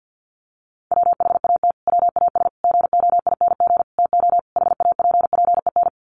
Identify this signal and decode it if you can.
Morse code, heard as a keyed tone, about 50 wpm, 700 Hz. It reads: WHAT WAS GOING TO HAPPEN
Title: Sets of 5 Words Encore 50wpm